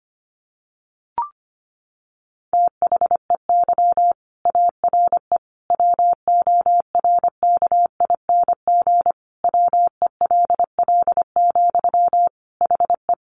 Morse code, heard as a keyed tone, about 25 wpm, 700 Hz. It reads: THEY ARE WORKING WELL, HE
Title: Sentences from Top 700 Words 25wpm